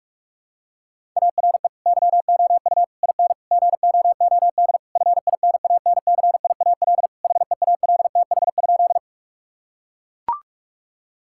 Morse code, heard as a keyed tone, about 42 wpm, 700 Hz. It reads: ARE YOU IN GOOD FINANCIAL HEALTH?
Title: Sentences from Top 1000 Words - Review Part 5 - 45wpm